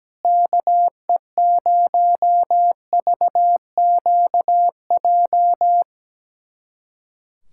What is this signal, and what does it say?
Morse code, heard as a keyed tone, about 17 wpm, 700 Hz. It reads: KE0VQJ
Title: USA Callsign DE Callsign 17wpm